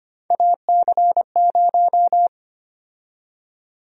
Morse code, heard as a keyed tone, about 25 wpm, 700 Hz. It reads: AC0
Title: USA Call Sign Prefixes - 25wpm